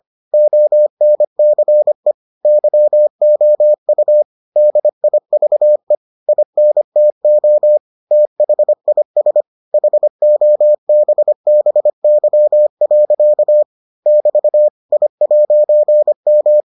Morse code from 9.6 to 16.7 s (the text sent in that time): HOBBY. <BT> I'M